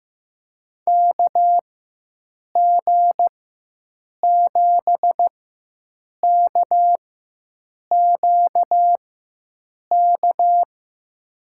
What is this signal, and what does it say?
Morse code, heard as a keyed tone, about 15 wpm, spaced as for 7 wpm, 700 Hz. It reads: KG7KQK
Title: USA Call Signs - 6 Characters - 4x Character Spacing - 15wpm